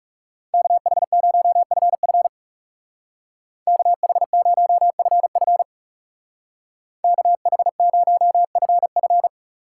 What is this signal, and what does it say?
Morse code, heard as a keyed tone, about 37 wpm, 700 Hz. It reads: KH0FF KH0FF KH0FF
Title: Speed Racing - USA 2x2 Call Signs 30wpm